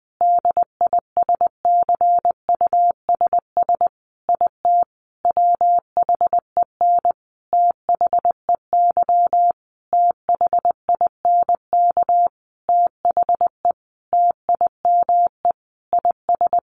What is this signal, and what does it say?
Morse code, heard as a keyed tone, about 19 wpm, 700 Hz. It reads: DISCUSS IT WHEN THEY THINK THE TIME IS
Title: Sentences from Top 1400 Words 20wpm